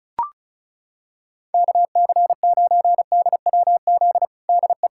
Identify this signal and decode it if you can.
Morse code, heard as a keyed tone, about 36 wpm, 700 Hz. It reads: KC9DWZ DE
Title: USA Callsign DE Callsign - 6 Characters - 35wpm